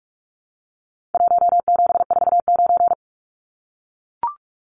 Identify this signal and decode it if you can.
Morse code, heard as a keyed tone, about 44 wpm, 700 Hz. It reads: 1749
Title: Numbers - Contest Style - 4-Digit 45wpm